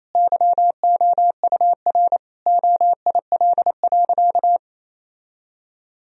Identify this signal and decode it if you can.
Morse code, heard as a keyed tone, about 28 wpm, 700 Hz. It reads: YOUR OIL.
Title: Sentences from Top 1000 Words - Review Part 4 - 28wpm